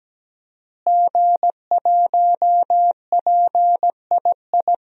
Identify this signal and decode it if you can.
Morse code, heard as a keyed tone, about 17 wpm, 700 Hz. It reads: G1PII
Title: DXCC-100 Call Signs - 5 Characters - Repeated 3x - 17wpm